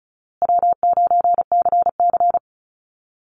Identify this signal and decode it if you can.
Morse code, heard as a keyed tone, about 34 wpm, 700 Hz. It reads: W9CC
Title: Easy Call Signs Repeated 3x - 35wpm